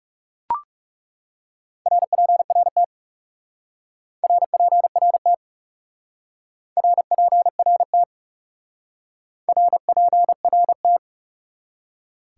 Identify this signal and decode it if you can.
Morse code, heard as a keyed tone, about 38 wpm, 700 Hz. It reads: RPRT RPRT RPRT RPRT